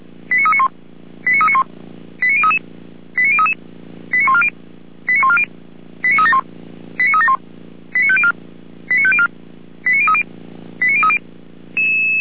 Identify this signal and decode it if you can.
Five-tone selective call (ZVEI), heard as quick runs of five tones, tones of about 70 ms. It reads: 89391 89391 89033 89033 89140 89140 89371 89371 89504 89504 89030 89030